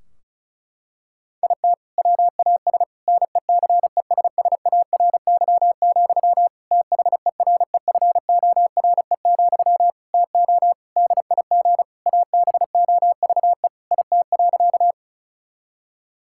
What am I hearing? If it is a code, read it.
Morse code, heard as a keyed tone, about 34 wpm, 700 Hz. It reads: IT WAS NECESSARY, THEREFORE, TO DIG ABOVE IT.
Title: Sentences from Top 1600 Words 35wpm